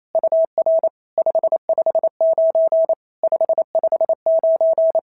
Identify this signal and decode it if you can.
Morse code, heard as a keyed tone, about 28 wpm, 650 Hz. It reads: UR 559 559
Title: SOTA QSO Practice 28wpm